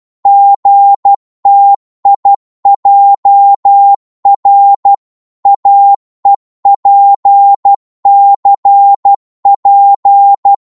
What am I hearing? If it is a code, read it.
Morse code, heard as a keyed tone, about 12 wpm, 800 Hz. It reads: GTIJR AEPCP
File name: Morse / ltr-12wpm